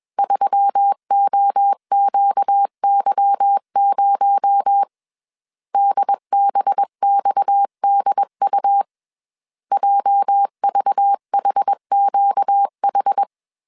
Morse code, heard as a keyed tone, about 21 wpm, 800 Hz. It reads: 3OQY0 DBXDU JVHQH
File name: CW / DET / 1 9 1500C S C 20wpm 2310